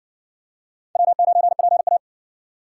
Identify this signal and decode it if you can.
Morse code, heard as a keyed tone, about 60 wpm, 700 Hz. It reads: W9PA